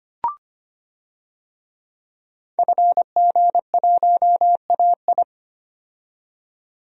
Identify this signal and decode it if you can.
Morse code, heard as a keyed tone, about 25 wpm, 700 Hz. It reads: FG1AI
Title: DXCC-100 Call Signs - 5 Characters - 25wpm